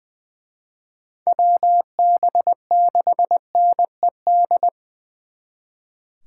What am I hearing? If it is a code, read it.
Morse code, heard as a keyed tone, about 24 wpm, 700 Hz. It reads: WB6NED
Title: Speed Racing - USA 2x3 Call Signs 20wpm